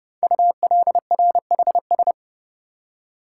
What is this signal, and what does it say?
Morse code, heard as a keyed tone, about 28 wpm, 700 Hz. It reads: ULRHS
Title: Code Groups - 5 Letters - 30wpm